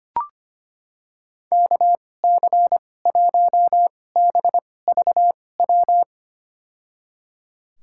Morse code, heard as a keyed tone, about 25 wpm, spaced as for 18 wpm, 700 Hz. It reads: KC1BVW
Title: USA Call Signs - 6 Characters - 2x Character Spacing - 25wpm